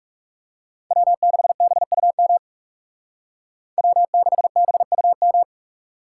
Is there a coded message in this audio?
Yes: Morse code from 0.9 to 5.4 s, heard as a keyed tone, about 38 wpm, 700 Hz. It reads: W6BUM W6BUM